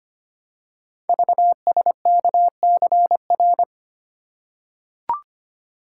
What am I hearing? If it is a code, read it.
Morse code, heard as a keyed tone, about 25 wpm, 700 Hz. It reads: VSKCR
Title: Code Groups - 5 Letters - 25wpm